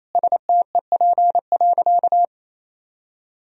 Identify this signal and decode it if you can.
Morse code, heard as a keyed tone, about 27 wpm, 690 Hz. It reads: STEP.
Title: Sentences from Top 1000 Words - Review Part 2 - 28wpm